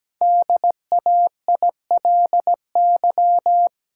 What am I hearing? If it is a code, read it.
Morse code, heard as a keyed tone, about 17 wpm, 700 Hz. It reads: DAILY
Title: Sentences from Top 1100 Words - New Word Drill Alt - 17wpm